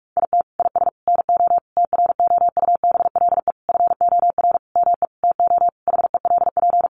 Morse code, heard as a keyed tone, about 44 wpm, 700 Hz. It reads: IT IS NO TROUBLE FOR ME TO HELP
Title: Sentences from Top 1200 Words 45wpm